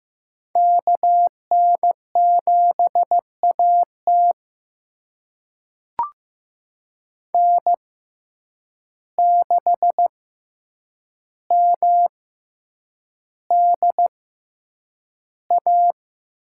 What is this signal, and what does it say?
Morse code, heard as a keyed tone, about 15 wpm, spaced as for 8 wpm, 700 Hz. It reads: KN7AT N6MDA
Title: USA Call Signs - 5 Characters - 6x Character Spacing - 15wpm